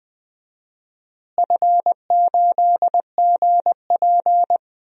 Morse code, heard as a keyed tone, about 20 wpm, 700 Hz. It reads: F8GP
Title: DXCC-100 Call Signs - 4 Characters - Repeated 2x - 20wpm